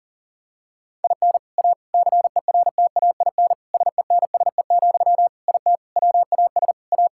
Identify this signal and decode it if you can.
Morse code, heard as a keyed tone, about 40 wpm, 700 Hz. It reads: IN A CERTAIN SENSE, IT WAS A
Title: Sentences from Top 1000 Words - Review Part 6 - 40wpm